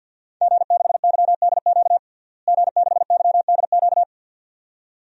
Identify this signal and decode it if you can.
Morse code, heard as a keyed tone, about 51 wpm, 700 Hz. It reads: G6YDQ G6YDQ
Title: DXCC-100 Call Signs - 5 Characters - Repeated 2x - 50wpm